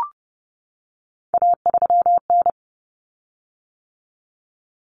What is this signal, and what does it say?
Morse code, heard as a keyed tone, about 30 wpm, 700 Hz. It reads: A3N
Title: Common Cut Numbers - 3-Digit 30wpm